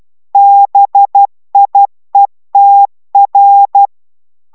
Morse code, heard as a keyed tone, about 12 wpm, 800 Hz. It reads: BIETR